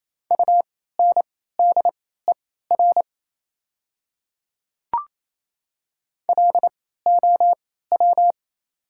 Morse code, heard as a keyed tone, about 28 wpm, spaced as for 16 wpm, 690 Hz. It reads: UNDER LOW
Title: 3x Character Spacing - Top 200 Words - 28wpm